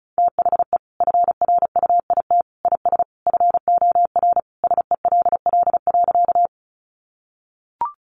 Morse code, heard as a keyed tone, about 34 wpm, 700 Hz. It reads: THE FRUIT IS FOR SELL.
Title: Sentences from Top 1300 Words 35wpm